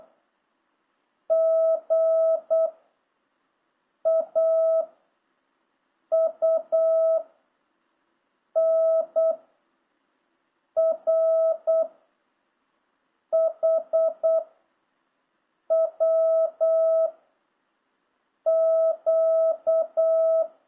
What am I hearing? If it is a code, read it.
Morse code, heard as a keyed tone, about 8 wpm, 660 Hz. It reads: G A U N R H W Q